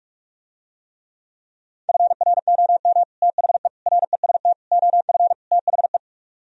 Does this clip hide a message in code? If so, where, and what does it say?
Yes: Morse code from 1.9 to 6.0 s, heard as a keyed tone, about 45 wpm, 700 Hz. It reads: FROM THE REST OF THE